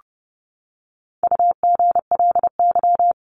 Morse code, heard as a keyed tone, about 30 wpm, 700 Hz. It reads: UGLY
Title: Top 1751-2000 Words 30wpm